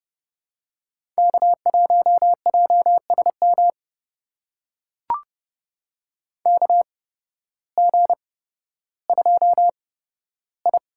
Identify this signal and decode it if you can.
Morse code, heard as a keyed tone, about 29 wpm, spaced as for 14 wpm, 700 Hz. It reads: K1JSM KG2I
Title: USA Call Signs - 8x Character Spacing - 30wpm